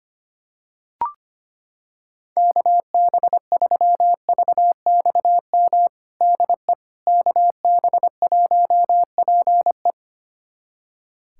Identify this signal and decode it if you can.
Morse code, heard as a keyed tone, about 25 wpm, 700 Hz. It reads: KB3VXM DE KB1PE
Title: USA Callsign DE Callsign 25wpm